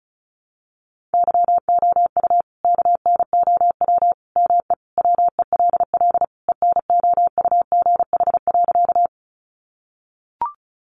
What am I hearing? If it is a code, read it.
Morse code, heard as a keyed tone, about 34 wpm, 700 Hz. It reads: YOU KNOW ME WELL ENOUGH.
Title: Sentences from Top 400 Words 35wpm